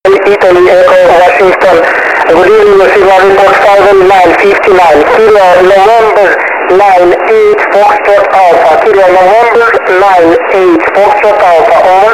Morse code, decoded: none